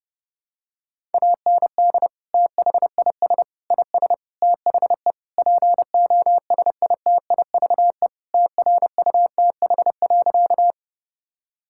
Morse code, heard as a keyed tone, about 30 wpm, 700 Hz. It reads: AND THIS IS THE POSITIVE TRUTH.
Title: Sentences from Top 1000 Words - Review Part 5 - 30wpm